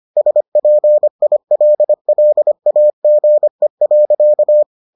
Morse code, heard as a keyed tone, about 25 wpm, 600 Hz. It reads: SPILLAGE.